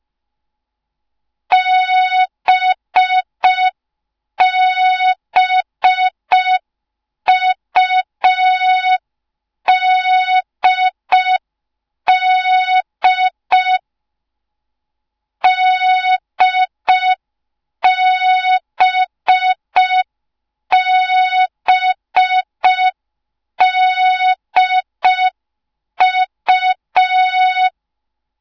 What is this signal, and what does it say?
Morse code, heard as a keyed tone, about 5 wpm, 3700 Hz. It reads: BBUDD DBBDU